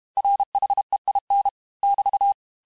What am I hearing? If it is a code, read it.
Morse code, heard as a keyed tone, about 32 wpm, 800 Hz. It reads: RHEIN <BT>
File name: cours-du-mercredi / audios / cours-868-teste-MOTS-vitesse-30mpm